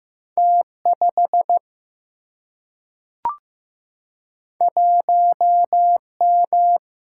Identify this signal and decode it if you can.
Morse code, heard as a keyed tone, about 15 wpm, 700 Hz. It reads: T 5 1 M
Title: Single-Letter Single-Number 15wpm